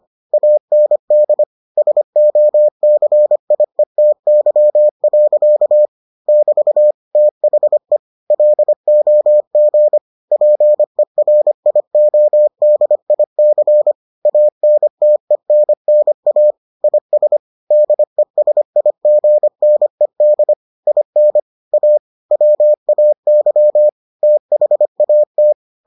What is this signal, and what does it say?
Morse code, heard as a keyed tone, about 25 wpm, 600 Hz. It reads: AND SOCIETY. <BT> THE LOG PERIODIC ANTENNA IS DESIGNED IN A WAY THAT